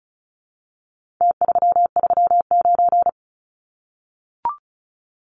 Morse code, heard as a keyed tone, about 34 wpm, 700 Hz. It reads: T339
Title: Numbers - Contest Style - 4-Digit 35wpm